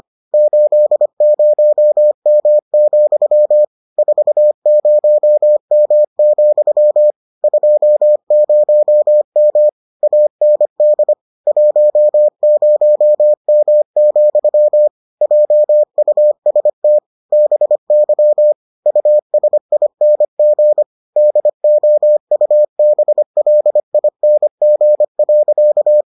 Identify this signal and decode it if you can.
Morse code, heard as a keyed tone, about 25 wpm, 600 Hz. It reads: 80M, 40M, 20M AND 10M, JUST BY USING DOUBLING.